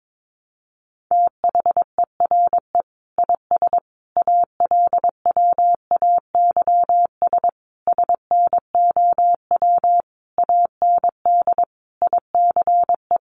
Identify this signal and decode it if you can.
Morse code, heard as a keyed tone, about 22 wpm, 700 Hz. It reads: THERE IS ALWAYS SNOW AND ICE
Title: Sentences from Top 1000 Words 22wpm